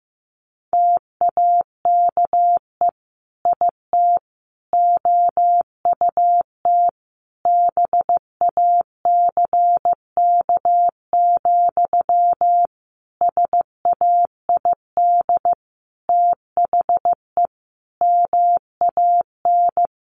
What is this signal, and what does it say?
Morse code, heard as a keyed tone, about 15 wpm, 700 Hz. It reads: TAKE IT OUT BACK, SAID THE MAN
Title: Sentences from Top 1500 Words 15wpm